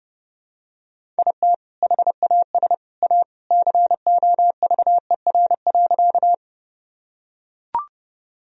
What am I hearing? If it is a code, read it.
Morse code, heard as a keyed tone, about 30 wpm, 700 Hz. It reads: IT HAS A COVER.
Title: Sentences from Top 1000 Words - Review Part 1 - 30wpm